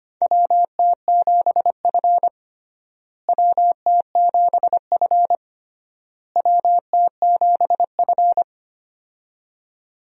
Morse code, heard as a keyed tone, about 25 wpm, 700 Hz. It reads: WT7F WT7F WT7F
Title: USA Call Signs - 4 Characters - Repeated 3x - 25wpm